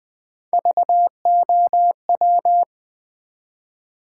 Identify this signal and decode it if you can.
Morse code, heard as a keyed tone, about 20 wpm, 700 Hz. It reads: VOW
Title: Build A Word - 1 Letter At A Time - 3-Letter Words - 20wpm